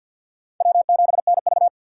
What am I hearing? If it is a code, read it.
Morse code, heard as a keyed tone, about 54 wpm, 700 Hz. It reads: W7NV
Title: USA Call Signs - 4 Characters - 3x Character Spacing - 50wpm